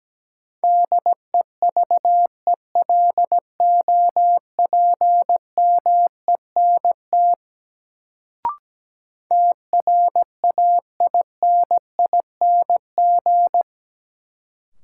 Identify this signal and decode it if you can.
Morse code, heard as a keyed tone, about 17 wpm, 700 Hz. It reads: DEVELOPMENT TRAINING
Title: Sentences from Top 1400 Words - New Word Drill Alt - 17wpm